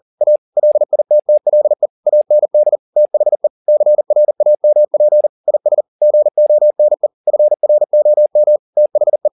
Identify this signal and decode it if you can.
Morse code, heard as a keyed tone, about 40 wpm, 600 Hz. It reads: A LITTLE AND THE CRAMP IS GONE FROM THE
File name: the old man and the sea 2 40wpm